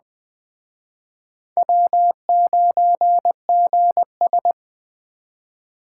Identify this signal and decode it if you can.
Morse code, heard as a keyed tone, about 21 wpm, 700 Hz. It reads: W9GS